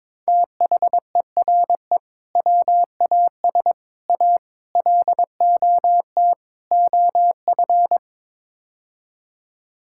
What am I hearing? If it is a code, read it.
Morse code, heard as a keyed tone, about 21 wpm, 700 Hz. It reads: THERE WAS A LOT OF
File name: practice / 22wpm / Build A Phrase - 5-Word Phrases - 22wpm